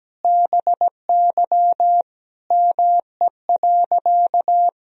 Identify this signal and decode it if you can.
Morse code, heard as a keyed tone, about 17 wpm, 700 Hz. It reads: BY ME.
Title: Sentences from Top 1000 Words - Review Part 1 - 17wpm